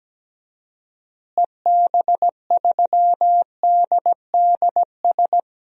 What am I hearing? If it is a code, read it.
Morse code, heard as a keyed tone, about 17 wpm, 700 Hz. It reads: EB3DDS
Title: DXCC-100 Call Signs - 6 Characters - Repeated 2x - 17wpm